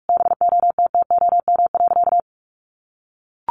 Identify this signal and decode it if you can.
Morse code, heard as a keyed tone, about 44 wpm, 700 Hz. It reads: BOTTOM.
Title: Easy Phrases 45wpm